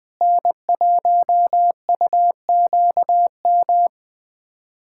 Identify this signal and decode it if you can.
Morse code, heard as a keyed tone, about 21 wpm, 700 Hz. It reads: N1UQM
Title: USA Callsign DE Callsign - 5 Characters - 20wpm